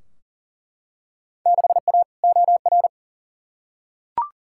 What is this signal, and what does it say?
Morse code, heard as a keyed tone, about 37 wpm, 700 Hz. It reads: 6A OR
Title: Field Day Exchanges - 40wpm